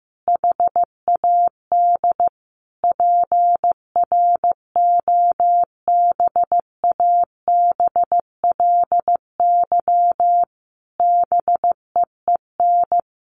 Morse code, heard as a keyed tone, about 15 wpm, 700 Hz. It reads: HAD PROBABLY BEEN
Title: Sentences from Top 700 Words 15wpm